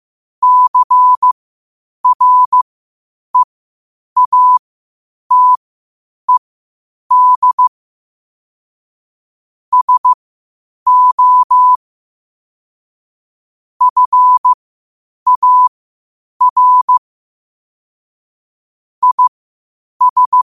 Morse code, heard as a keyed tone, about 15 wpm, spaced as for 8 wpm, 1000 Hz. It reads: CREATED SO FAR IS
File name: qotdfiles / 8wpm